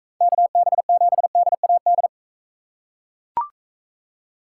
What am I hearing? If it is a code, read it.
Morse code, heard as a keyed tone, about 44 wpm, 700 Hz. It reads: KB7DAD